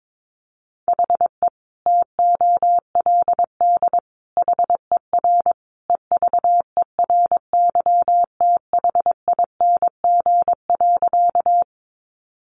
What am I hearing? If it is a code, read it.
Morse code, heard as a keyed tone, about 22 wpm, 700 Hz. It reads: HE TOLD HER EVERYTHING.
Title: Sentences from Top 700 Words 22wpm